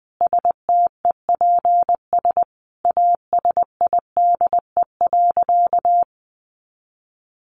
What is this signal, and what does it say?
Morse code, heard as a keyed tone, about 20 wpm, 700 Hz. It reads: STEPS ASIDE.
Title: A Sound Of Thunder 20wpm